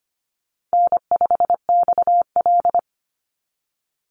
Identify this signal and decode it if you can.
Morse code, heard as a keyed tone, about 25 wpm, 700 Hz. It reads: N5XL
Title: NA SOTA Chaser Callsigns 25wpm